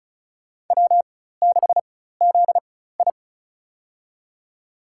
Morse code, heard as a keyed tone, about 36 wpm, spaced as for 16 wpm, 700 Hz. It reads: W6ZI
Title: USA Call Signs - 4 Characters - 4x Character Spacing - 35wpm